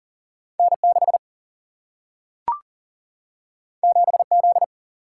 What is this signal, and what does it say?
Morse code, heard as a keyed tone, about 40 wpm, 700 Hz. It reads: N6 7Z
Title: Single-Letter Single-Number 40wpm